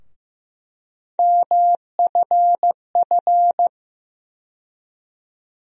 Morse code, heard as a keyed tone, about 15 wpm, 700 Hz. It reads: MFF
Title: USA Call Sign Postfixes - 3 Letters 15wpm